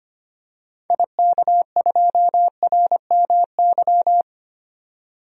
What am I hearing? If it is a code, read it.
Morse code, heard as a keyed tone, about 25 wpm, 700 Hz. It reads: IK2RMY
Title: DXCC-100 Call Signs - 6 Characters - 25wpm